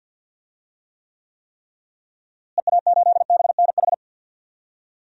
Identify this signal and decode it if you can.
Morse code, heard as a keyed tone, about 46 wpm, 700 Hz. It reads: EA8BNH
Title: DXCC-100 Call Signs - 6 Characters - Repeated 3x - 50wpm